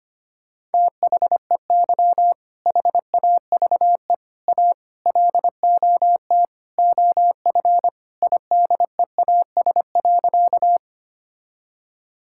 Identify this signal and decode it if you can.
Morse code, heard as a keyed tone, about 25 wpm, 700 Hz. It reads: THEY HAVE A LOT OF IDEAS.
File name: practice / 25wpm / Sentences with 4-Word Phrases - 25wpm